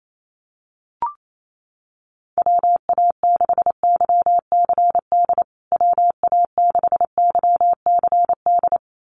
Morse code, heard as a keyed tone, about 28 wpm, 700 Hz. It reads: WA6YCD WA6YCD
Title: USA Call Signs Repeated 2x 28wpm